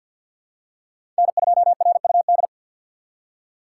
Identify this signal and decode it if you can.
Morse code, heard as a keyed tone, about 47 wpm, 700 Hz. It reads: N2RUD
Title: USA Call Signs - 5 Characters - 2x Character Spacing - 50wpm